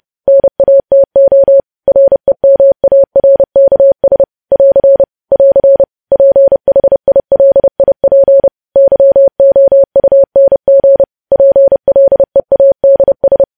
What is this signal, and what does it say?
Morse code, heard as a keyed tone, about 30 wpm, 550 Hz. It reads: NATO REMARKS <AR> <AR> PHILIP YOUNG PLEADS